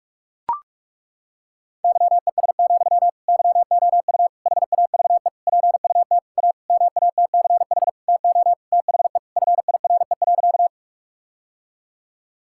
Morse code, heard as a keyed tone, about 41 wpm, 700 Hz. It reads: YES, YOU HAVE PUT A MATCH TO THE FIRE.
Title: Sentences from Top 1600 Words 45wpm